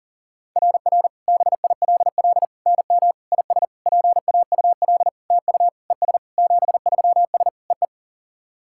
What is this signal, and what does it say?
Morse code, heard as a keyed tone, about 40 wpm, 700 Hz. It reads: RR BILL NM IS PAUL TU ES 73S EE
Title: SOTA QSO Practice 40wpm